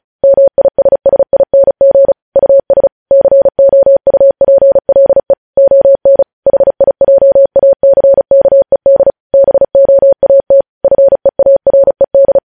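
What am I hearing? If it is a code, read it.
Morse code, heard as a keyed tone, about 36 wpm, 550 Hz. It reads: MISSING US COUPLE ON HIJACKED BOAT FEARED
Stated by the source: Morse code practice reading news headlines at 35 words per minute.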